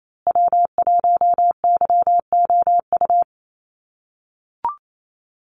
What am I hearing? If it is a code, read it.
Morse code, heard as a keyed tone, about 28 wpm, 700 Hz. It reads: W1YOU
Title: USA Call Signs Repeated 3x 28wpm